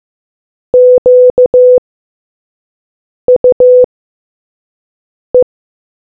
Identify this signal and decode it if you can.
Morse code, heard as a keyed tone, about 15 wpm, spaced as for 5 wpm, 500 Hz. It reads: QUE